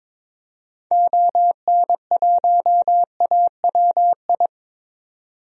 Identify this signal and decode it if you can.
Morse code, heard as a keyed tone, about 22 wpm, 700 Hz. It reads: ON1AWI
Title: DXCC-100 Call Signs - 6 Characters - 22wpm